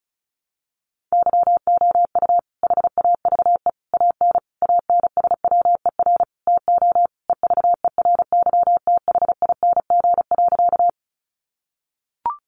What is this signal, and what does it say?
Morse code, heard as a keyed tone, about 34 wpm, 700 Hz. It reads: YOU HAVE AN ANSWER TO EVERYTHING.
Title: Sentences from Top 700 Words 35wpm